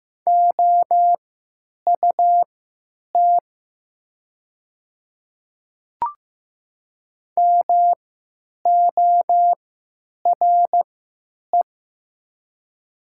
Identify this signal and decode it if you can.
Morse code, heard as a keyed tone, about 15 wpm, spaced as for 9 wpm, 700 Hz. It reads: OUT MORE
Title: 3x Character Spacing - Top 100 Words - 15wpm